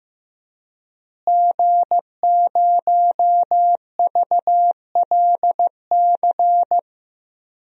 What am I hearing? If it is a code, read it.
Morse code, heard as a keyed tone, about 15 wpm, 700 Hz. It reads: G0VLC
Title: DXCC-100 Call Signs - 5 Characters - Repeated 3x - 15wpm